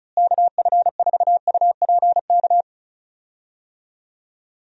Morse code, heard as a keyed tone, about 36 wpm, 700 Hz. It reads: KF4UPK
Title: USA Call Signs - 6 Characters - Repeated 3x - 35wpm